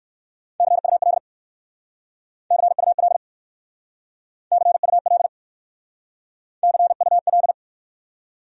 Morse code, heard as a keyed tone, about 59 wpm, 700 Hz. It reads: CUL CUL CUL CUL